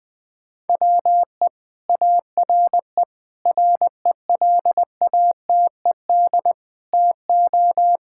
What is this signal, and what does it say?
Morse code, heard as a keyed tone, about 20 wpm, 700 Hz. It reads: WE ARE RELATED TO
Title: Sentences from Top 1000 Words - Review Part 5 - 20wpm